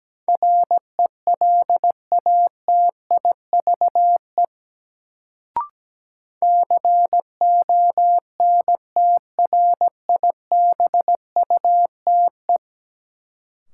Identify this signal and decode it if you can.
Morse code, heard as a keyed tone, about 17 wpm, 700 Hz. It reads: RELATIVE CONTRIBUTE
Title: Sentences from Top 1400 Words - New Word Drill - 17wpm